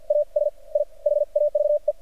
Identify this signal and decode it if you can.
Morse code, heard as a keyed tone, about 50 wpm, spaced as for 41 wpm, 590 Hz. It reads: AS I HAVE
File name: archived / CRSnet